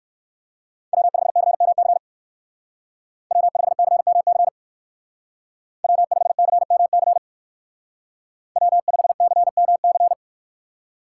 Morse code, heard as a keyed tone, about 58 wpm, 700 Hz. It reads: W5CMC W5CMC W5CMC W5CMC